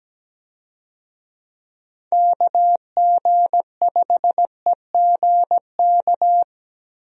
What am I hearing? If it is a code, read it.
Morse code, heard as a keyed tone, about 17 wpm, 700 Hz. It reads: KG5EGK
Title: USA Call Signs - 6 Characters - Repeated 2x - 17wpm